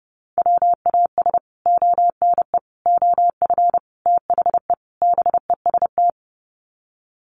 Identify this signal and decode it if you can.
Morse code, heard as a keyed tone, about 30 wpm, 700 Hz. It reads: WAS ONE OF THE BEST
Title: Sets of 5 Words Encore2 30wpm